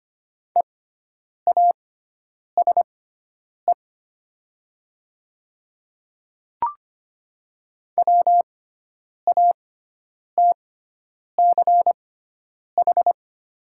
Morse code, heard as a keyed tone, about 25 wpm, spaced as for 9 wpm, 700 Hz. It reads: EASE WATCH
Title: 6x Character Spacing - Top 300 Words - 25wpm